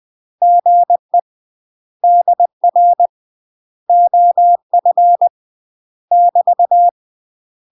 Morse code, heard as a keyed tone, about 20 wpm, 700 Hz. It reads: GE DR OF <BT>
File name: 20WPM